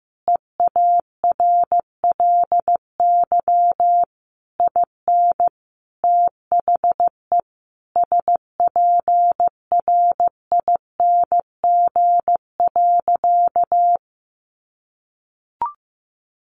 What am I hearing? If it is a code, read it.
Morse code, heard as a keyed tone, about 14 wpm, 700 Hz. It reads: EARLY IN THE SPRING.
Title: Sentences from Top 1200 Words 15wpm